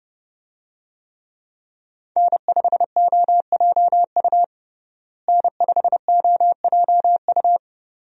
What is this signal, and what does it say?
Morse code, heard as a keyed tone, about 32 wpm, 700 Hz. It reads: N5OJU N5OJU
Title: USA Call Signs - 5 Characters - Repeated 2x - 30wpm